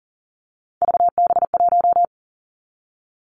Morse code, heard as a keyed tone, about 39 wpm, 700 Hz. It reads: VB1
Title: Three Challenging Letter-Number Combinations 40wpm